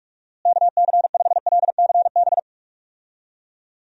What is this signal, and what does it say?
Morse code, heard as a keyed tone, about 43 wpm, 700 Hz. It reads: KC5LCB
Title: USA Call Signs - 6 Characters - Repeated 2x - 45wpm